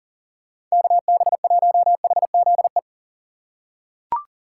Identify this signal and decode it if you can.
Morse code, heard as a keyed tone, about 40 wpm, 700 Hz. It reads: KB1HZE
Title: USA Call Signs - 6 Characters - 2x Character Spacing - 40wpm